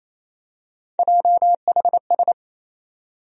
Morse code, heard as a keyed tone, about 27 wpm, 700 Hz. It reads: JHS
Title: USA Call Sign Postfixes - 3 Letters 28wpm